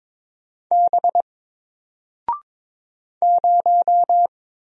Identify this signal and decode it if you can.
Morse code, heard as a keyed tone, about 22 wpm, 700 Hz. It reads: B 0
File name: Single-Letter-Number-22wpm